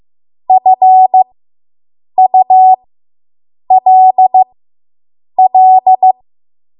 Morse code, heard as a keyed tone, about 15 wpm, spaced as for 7 wpm, 750 Hz. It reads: FULL